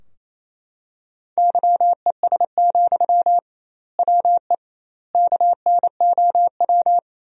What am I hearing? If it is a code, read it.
Morse code, heard as a keyed tone, about 28 wpm, 700 Hz. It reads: YES, WE KNOW
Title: 2x Word-Spacing - Sentences from Top 500 Words - 28wpm